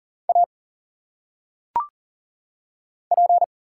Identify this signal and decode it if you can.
Morse code, heard as a keyed tone, about 40 wpm, 700 Hz. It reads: A P
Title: Single Letters - Phonetic Pronunciation 40wpm